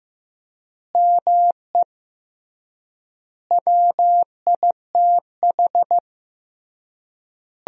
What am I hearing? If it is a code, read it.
Morse code, heard as a keyed tone, about 16 wpm, 700 Hz. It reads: ME WITH
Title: 3x Word-Spacing - Sentences from Top 200 Words - 15wpm